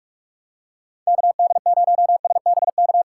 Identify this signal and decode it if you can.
Morse code, heard as a keyed tone, about 46 wpm, 700 Hz. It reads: KD0SBK
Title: USA Call Signs - 3x Character Spacing - 45wpm